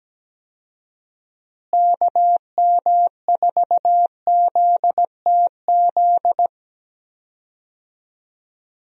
Morse code, heard as a keyed tone, about 17 wpm, 700 Hz. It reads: KM4ZTZ
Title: USA Call Signs - 6 Characters - Repeated 3x - 17wpm